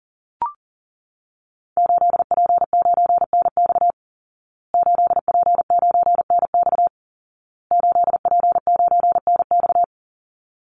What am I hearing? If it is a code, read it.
Morse code, heard as a keyed tone, about 40 wpm, 700 Hz. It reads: 8P9NX 8P9NX 8P9NX
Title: Call Signs Repeated 2x - 40wpm